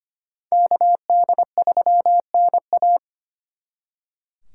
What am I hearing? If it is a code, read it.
Morse code, heard as a keyed tone, about 33 wpm, 700 Hz. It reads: KD3NA